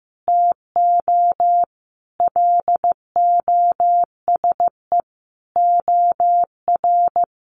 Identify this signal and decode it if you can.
Morse code, heard as a keyed tone, about 15 wpm, 700 Hz. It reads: TO LOSE OR
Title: Sentences from Top 1100 Words 15wpm